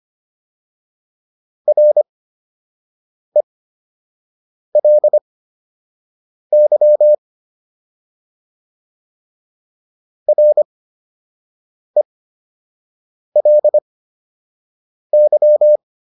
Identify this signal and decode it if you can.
Morse code, heard as a keyed tone, about 25 wpm, spaced as for 6 wpm, 600 Hz. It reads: RELY RELY